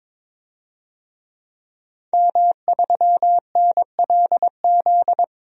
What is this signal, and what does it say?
Morse code, heard as a keyed tone, about 22 wpm, 700 Hz. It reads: M3NLZ
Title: DXCC-100 Call Signs - 5 Characters - 22wpm